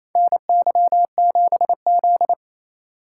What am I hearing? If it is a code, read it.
Morse code, heard as a keyed tone, about 28 wpm, 700 Hz. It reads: NY7Z
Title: K1USN SST QSO Practice - 28wpm